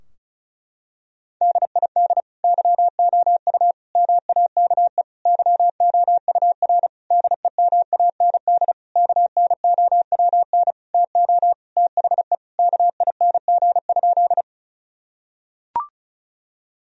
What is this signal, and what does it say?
Morse code, heard as a keyed tone, about 36 wpm, 700 Hz. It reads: DID YOU MAKE YOUR DEMAND KNOWN TO THE KING?
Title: Sentences from Top 1000 Words - Review Part 4 - 35wpm